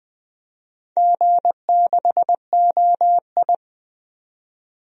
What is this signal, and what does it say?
Morse code, heard as a keyed tone, about 20 wpm, 700 Hz. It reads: G6OI
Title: DXCC-100 Call Signs - 4 Characters - 20wpm